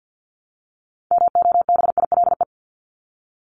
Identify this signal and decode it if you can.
Morse code, heard as a keyed tone, about 48 wpm, 700 Hz. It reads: MOBILE
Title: Top 1751-2000 Words 50wpm